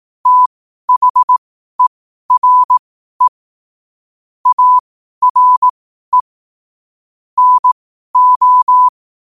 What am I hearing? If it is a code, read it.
Morse code, heard as a keyed tone, about 18 wpm, spaced as for 12 wpm, 1000 Hz. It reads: THERE ARE NO
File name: qotdfiles / 12wpm